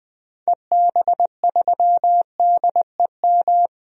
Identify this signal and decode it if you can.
Morse code, heard as a keyed tone, about 20 wpm, 700 Hz. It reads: EB3DEM
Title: DXCC-100 Call Signs - 6 Characters - 20wpm